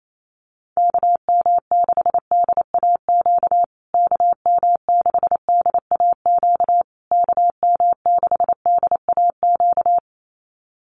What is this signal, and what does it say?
Morse code, heard as a keyed tone, about 28 wpm, 700 Hz. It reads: KM6DAQ KM6DAQ KM6DAQ
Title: USA Call Signs Repeated 3x 28wpm